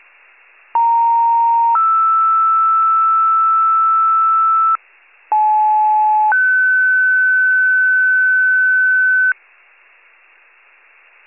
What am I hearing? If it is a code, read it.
Two-tone page, heard as a one-second tone, then a three-second tone, about 930 then 1320 Hz, about 860 then 1500 Hz. It is heard from 0.7 to 9.3 s.
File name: wcalert